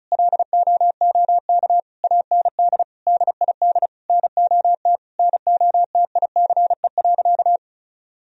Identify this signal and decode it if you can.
Morse code, heard as a keyed tone, about 36 wpm, 700 Hz. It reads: LOOK AND DID NOT NOTICE.
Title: Sentences from Top 1000 Words - Review Part 2 - 35wpm